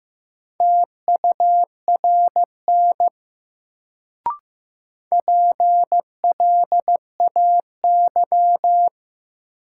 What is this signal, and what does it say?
Morse code, heard as a keyed tone, about 15 wpm, 700 Hz. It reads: TURN PLAY
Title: Sentences from Top 200 Words - New Word Drill - 15wpm